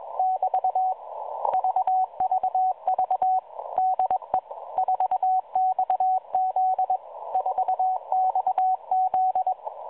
Morse code, heard as a keyed tone, about 21 wpm, 750 Hz, with radio noise. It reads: <BT> VVV DE 4XZ 4XZ